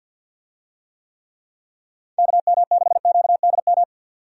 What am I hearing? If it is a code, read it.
Morse code, heard as a keyed tone, about 45 wpm, 700 Hz. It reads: KM6QDM
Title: USA Call Signs - 6 Characters - 50wpm